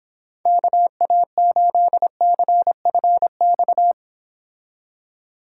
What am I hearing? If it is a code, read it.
Morse code, heard as a keyed tone, about 26 wpm, 700 Hz. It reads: KA8CFX